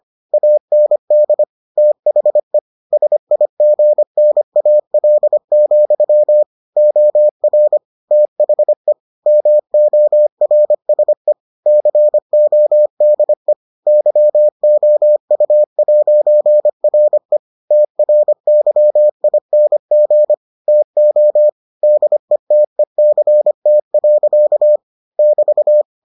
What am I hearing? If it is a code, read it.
Morse code, heard as a keyed tone, about 25 wpm, 600 Hz. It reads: AND THE SIGNAL, OR THE MORSE CODE YOU'RE TRYING TO DETECT. <BT>